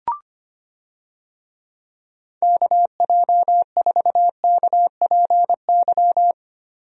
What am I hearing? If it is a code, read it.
Morse code, heard as a keyed tone, about 25 wpm, 700 Hz. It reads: KJ4KPY
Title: USA Call Signs - 6 Characters - Repeated 2x - 25wpm